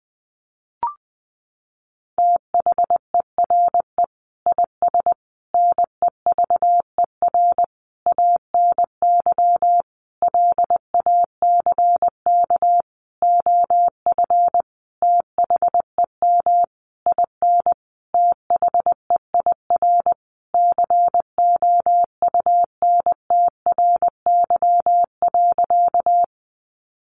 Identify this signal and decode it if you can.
Morse code, heard as a keyed tone, about 20 wpm, 700 Hz. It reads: THERE IS NEVER ANY LACK OF THEM IN THEIR COUNTRY.
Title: Sentences from Top 1100 Words 20wpm